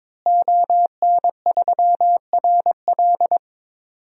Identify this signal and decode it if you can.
Morse code, heard as a keyed tone, about 23 wpm, 700 Hz. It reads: ON3RL
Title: DXCC-100 Call Signs - 5 Characters - Repeated 3x - 22wpm